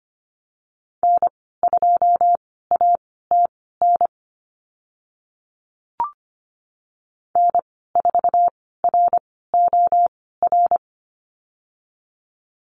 Morse code, heard as a keyed tone, about 25 wpm, spaced as for 16 wpm, 690 Hz. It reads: N2ATN N4ROR
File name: Lesson 15g - Number 5 - Call Signs 25wpm